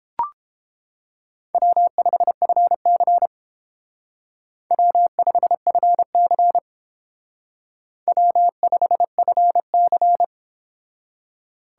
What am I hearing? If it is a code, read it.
Morse code, heard as a keyed tone, about 29 wpm, 700 Hz. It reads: W5FC W5FC W5FC